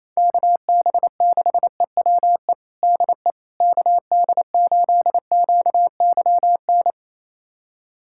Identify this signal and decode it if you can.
Morse code, heard as a keyed tone, about 28 wpm, 700 Hz. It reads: KB6EWE DE KD8QYN
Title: USA Callsign DE Callsign - 6 Characters - 28wpm